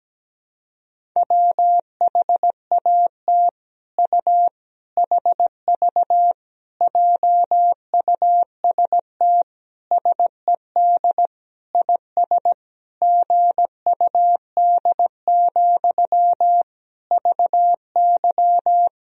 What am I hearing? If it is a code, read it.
Morse code, heard as a keyed tone, about 17 wpm, 700 Hz. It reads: WHAT U HV JUST SED IS GUD, VY
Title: Sentences from Top 200 Words and Abbreviations 17wpm